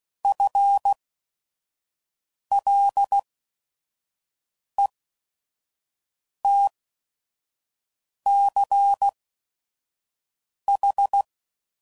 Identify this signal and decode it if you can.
Morse code, heard as a keyed tone, about 16 wpm, spaced as for 5 wpm, 800 Hz. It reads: FLETCH
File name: Prefixes / By First Letter / Fs